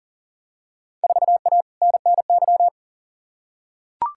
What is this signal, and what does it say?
Morse code, heard as a keyed tone, about 40 wpm, 700 Hz. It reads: 4A NNY
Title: Field Day Exchanges Encore - 40wpm